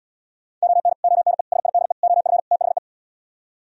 Morse code, heard as a keyed tone, about 77 wpm, 700 Hz. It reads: ON ONE SIDE OF THE